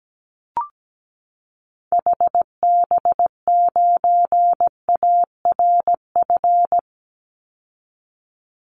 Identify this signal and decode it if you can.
Morse code, heard as a keyed tone, about 17 wpm, 700 Hz. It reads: HB9ARF
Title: Call Signs Repeated 3x - 17wpm